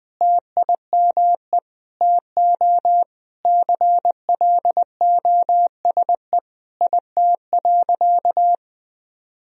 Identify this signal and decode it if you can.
Morse code, heard as a keyed tone, about 19 wpm, 700 Hz. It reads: TIME TO CLOSE IT.
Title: Sentences from Top 1000 Words - Review Part 1 - 20wpm